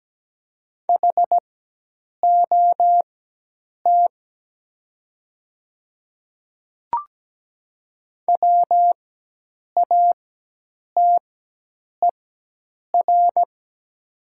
Morse code, heard as a keyed tone, about 17 wpm, spaced as for 8 wpm, 700 Hz. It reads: HOT WATER
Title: 4x Character Spacing - Top 300 Words - 17wpm